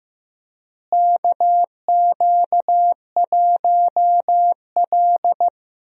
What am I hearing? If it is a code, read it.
Morse code, heard as a keyed tone, about 15 wpm, 700 Hz. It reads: KQ1L
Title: USA Call Signs - 4 Characters - Repeated 3x - 15wpm